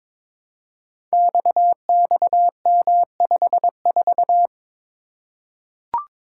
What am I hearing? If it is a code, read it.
Morse code, heard as a keyed tone, about 22 wpm, 700 Hz. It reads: XXM54
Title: Code Groups - 5 Letter:Numbers - 22wpm